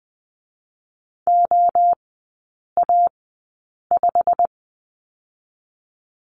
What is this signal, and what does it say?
Morse code, heard as a keyed tone, about 19 wpm, 700 Hz. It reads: O A 5
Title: 121 Lesson 20e - Number 3 - Three Characters 20wpm